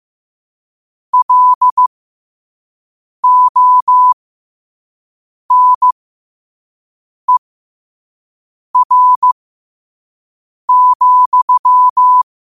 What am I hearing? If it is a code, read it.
Morse code, heard as a keyed tone, about 15 wpm, spaced as for 5 wpm, 1000 Hz. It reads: LONER,
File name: qotdfiles / 5wpm